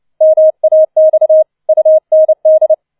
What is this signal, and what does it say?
Morse code, heard as a keyed tone, about 29 wpm, 630 Hz. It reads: MAX UND
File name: 004 Vierter Streich 30wpm